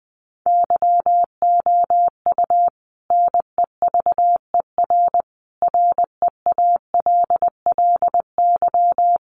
Morse code, heard as a keyed tone, about 19 wpm, 700 Hz. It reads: YOU NEVER REALLY
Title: Sentences from Top 1500 Words 20wpm